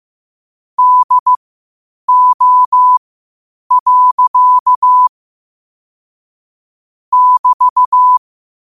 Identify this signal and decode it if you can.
Morse code, heard as a keyed tone, about 15 wpm, spaced as for 8 wpm, 1000 Hz. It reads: DO. <BT>
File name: qotdfiles / 8wpm